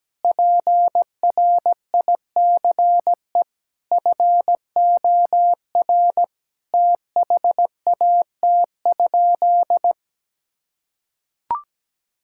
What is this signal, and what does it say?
Morse code, heard as a keyed tone, about 17 wpm, 700 Hz. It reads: PRICE FOR THAT?
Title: Sentences from Top 1000 Words - Review Part 6 - 17wpm